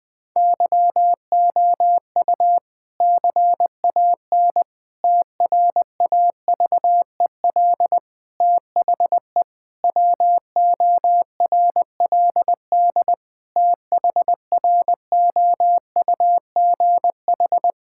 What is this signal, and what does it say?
Morse code, heard as a keyed tone, about 20 wpm, 700 Hz. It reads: YOU CAN TRAVEL THE WORLD THROUGH
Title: Difficult Sentences with Hints - 20wpm